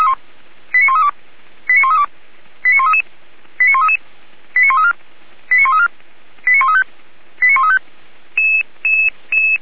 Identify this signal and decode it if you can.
Five-tone selective call (ZVEI), heard as quick runs of five tones, tones of about 70 ms. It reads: 89132 89132 89133 89133 89134 89134 89135 89135